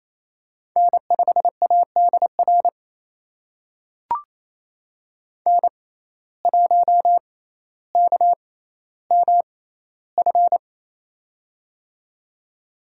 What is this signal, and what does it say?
Morse code, heard as a keyed tone, about 28 wpm, spaced as for 14 wpm, 700 Hz. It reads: N5ADR N1KMF
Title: USA Call Signs - 5 Characters - 6x Character Spacing - 28wpm